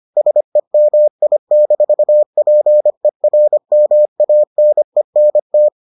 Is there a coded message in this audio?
Yes: Morse code from 0.2 to 5.7 s, heard as a keyed tone, about 25 wpm, 600 Hz. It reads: SEMI-PERMANENT